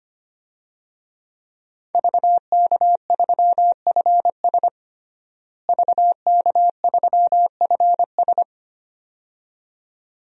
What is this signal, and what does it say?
Morse code, heard as a keyed tone, about 25 wpm, 700 Hz. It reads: VK3FS VK3FS
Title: USA Call Signs - 5 Characters - Repeated 2x - 25wpm